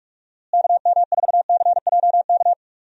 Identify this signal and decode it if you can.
Morse code, heard as a keyed tone, about 46 wpm, 700 Hz. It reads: KM4CJK
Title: USA Call Signs - 6 Characters - Repeated 3x - 45wpm